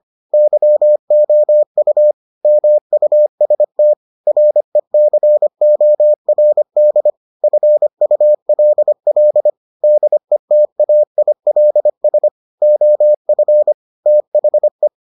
Morse code, heard as a keyed tone, about 25 wpm, 600 Hz. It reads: YOU MUST RECORD FULL DETAILS OF THE